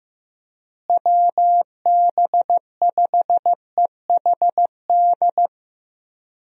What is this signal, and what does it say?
Morse code, heard as a keyed tone, about 16 wpm, 700 Hz. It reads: WB5EHD
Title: USA Call Signs - 6 Characters - Repeated 2x - 15wpm